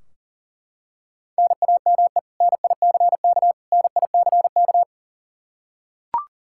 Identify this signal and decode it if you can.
Morse code, heard as a keyed tone, about 40 wpm, 700 Hz. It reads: NAME NICK NICK
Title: Practice QSO - Exchange 1 - Beginning - 40wpm